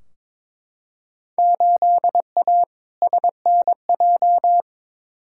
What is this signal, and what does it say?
Morse code, heard as a keyed tone, about 21 wpm, 700 Hz. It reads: 8A SNJ
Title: Field Day Exchanges Encore - 22wpm